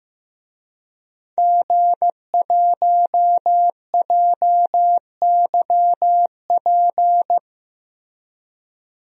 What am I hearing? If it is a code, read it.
Morse code, heard as a keyed tone, about 15 wpm, 700 Hz. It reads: G1JYP
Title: DXCC-100 Call Signs - 5 Characters - Repeated 3x - 15wpm